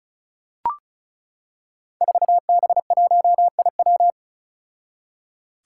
Morse code, heard as a keyed tone, about 36 wpm, 700 Hz. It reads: 4B1IW
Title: Five Challenging Letter-Number Combinations 35wpm